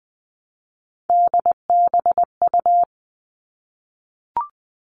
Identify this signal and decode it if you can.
Morse code, heard as a keyed tone, about 20 wpm, 700 Hz. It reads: DBU
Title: All Cut Numbers - 3-Digit 20wpm